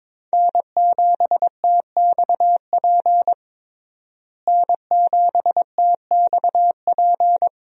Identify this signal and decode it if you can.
Morse code, heard as a keyed tone, about 22 wpm, 700 Hz. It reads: N7TXP N7TXP
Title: USA Call Signs - 5 Characters - Repeated 2x - 22wpm